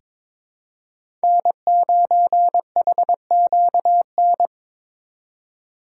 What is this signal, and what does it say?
Morse code, heard as a keyed tone, about 22 wpm, 700 Hz. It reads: N9HQN
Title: USA Call Signs - 5 Characters - 6x Character Spacing - 22wpm